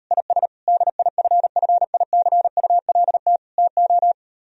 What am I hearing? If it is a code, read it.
Morse code, heard as a keyed tone, about 38 wpm, 700 Hz. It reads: IS DIFFICULT TO